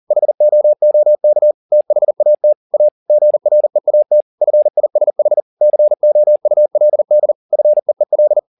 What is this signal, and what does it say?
Morse code, heard as a keyed tone, about 40 wpm, 600 Hz. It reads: HOOK THAT A GREAT FISH COULD FEEL